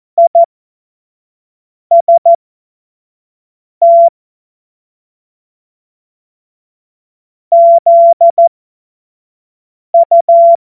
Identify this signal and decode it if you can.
Morse code, heard as a keyed tone, about 14 wpm, spaced as for 5 wpm, 680 Hz. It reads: IST ZU
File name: Morsekurs / Witze / Witz 20 Februar 2024 25 F70